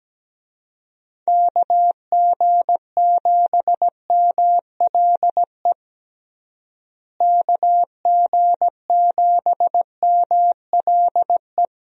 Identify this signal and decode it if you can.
Morse code, heard as a keyed tone, about 17 wpm, 700 Hz. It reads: KG7MLE KG7MLE
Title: USA Call Signs - 6 Characters - Repeated 2x - 17wpm